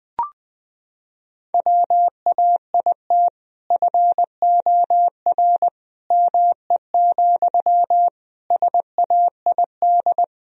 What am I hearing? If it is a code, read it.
Morse code, heard as a keyed tone, about 21 wpm, 700 Hz. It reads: WAIT FOR ME, SAID
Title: Sentences from Top 1000 Words - Review Part 4 - 20wpm